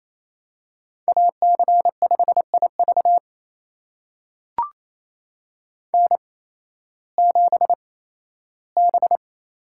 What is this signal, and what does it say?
Morse code, heard as a keyed tone, about 28 wpm, spaced as for 15 wpm, 700 Hz. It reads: AC5IV N7B
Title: USA Call Signs - 5 Characters - 8x Character Spacing - 28wpm